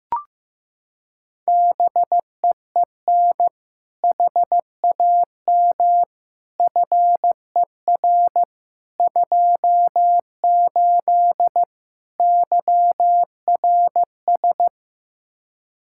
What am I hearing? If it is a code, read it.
Morse code, heard as a keyed tone, about 15 wpm, 700 Hz. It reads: BEEN HAM FER 28 YRS
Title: Practice QSO - Exchange 3 - More Info - 15wpm